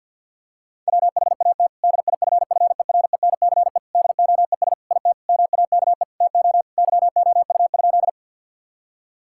Morse code, heard as a keyed tone, about 52 wpm, 700 Hz. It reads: WHAT DIFFERENCE DOES IT MAKE TO YOU?
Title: Sentences from Top 1000 Words - Review Part 5 - 50wpm